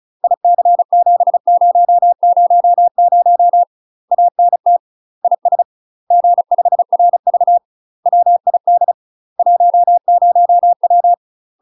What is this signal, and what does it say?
Morse code, heard as a keyed tone, about 34 wpm, 700 Hz. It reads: IC7000 ANT IS G5RV WID 10W
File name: UK MASTER QSOs TRAINING / 35WPM / UK MASTER QSOs 0043